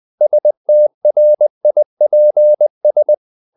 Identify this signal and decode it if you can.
Morse code, heard as a keyed tone, about 20 wpm, 600 Hz. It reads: STRIPS
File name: the old man and the sea / the old man and the sea part 1 20wpm